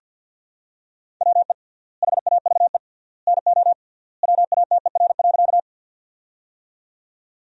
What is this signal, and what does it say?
Morse code, heard as a keyed tone, about 50 wpm, 700 Hz. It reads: WE HAVE NO WATER.
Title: 3x Word-Spacing - Sentences from Top 100 Words - 50wpm